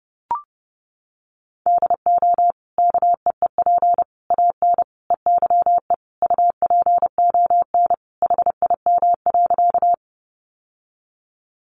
Morse code, heard as a keyed tone, about 30 wpm, 700 Hz. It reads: DO KEEP AN EYE UPON HIM.
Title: Sentences from Top 1100 Words 30wpm